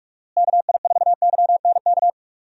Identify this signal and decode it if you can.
Morse code, heard as a keyed tone, about 41 wpm, 700 Hz. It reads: KI4YNK
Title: USA Call Signs - 6 Characters - Repeated 3x - 45wpm